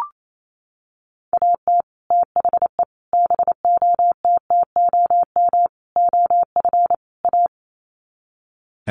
Morse code, heard as a keyed tone, about 28 wpm, 700 Hz. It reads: AT THE BOTTOM OF A
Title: Sets of 5 Words Encore2 28wpm